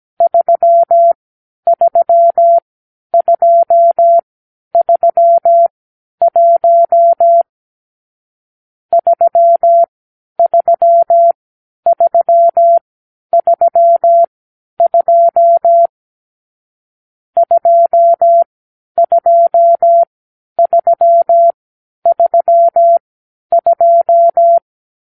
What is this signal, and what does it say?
Morse code, heard as a keyed tone, about 17 wpm, spaced as for 10 wpm, 680 Hz. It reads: 33231 33332 22332
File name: k7qo / Lesson 44 (1-3)